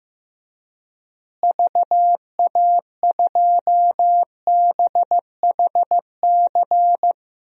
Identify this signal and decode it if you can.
Morse code, heard as a keyed tone, about 15 wpm, 700 Hz. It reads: VA2BHC
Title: DXCC-100 Call Signs - 6 Characters - Repeated 2x - 15wpm